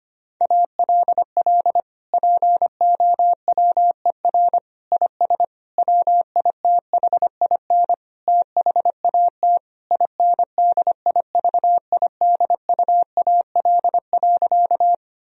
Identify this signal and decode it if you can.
Morse code, heard as a keyed tone, about 25 wpm, 700 Hz. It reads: ALL POWER IS WITHIN THAT INDIVIDUAL.
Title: Sentences from Top 1000 Words - Review Part 4 - 25wpm